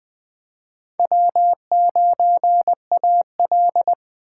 Morse code, heard as a keyed tone, about 20 wpm, 700 Hz. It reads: W9AL
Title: USA Call Signs - 4x Character Spacing - 20wpm